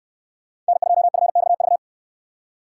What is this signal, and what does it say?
Morse code, heard as a keyed tone, about 66 wpm, 700 Hz. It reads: N2FXV